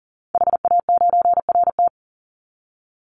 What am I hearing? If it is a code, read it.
Morse code, heard as a keyed tone, about 40 wpm, 700 Hz. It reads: HA9RT
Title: Call Signs - 40wpm